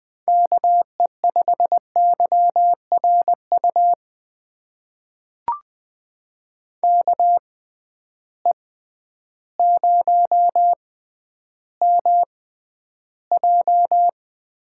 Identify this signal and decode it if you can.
Morse code, heard as a keyed tone, about 20 wpm, spaced as for 11 wpm, 700 Hz. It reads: KE5YRU KE0MJ
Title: USA Call Signs - 6 Characters - 6x Character Spacing - 20wpm